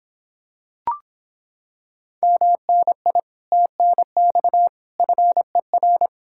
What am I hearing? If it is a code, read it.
Morse code, heard as a keyed tone, about 26 wpm, 700 Hz. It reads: MNI TNX FER